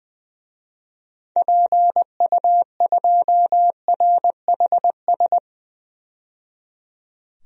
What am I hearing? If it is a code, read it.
Morse code, heard as a keyed tone, about 20 wpm, 700 Hz. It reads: PU2RHS
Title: DXCC-100 Call Signs - 6 Characters - 20wpm